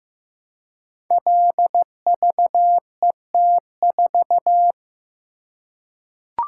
Morse code, heard as a keyed tone, about 15 wpm, 700 Hz. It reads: LVET4
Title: Code Groups - 5 Letter:Numbers - 15wpm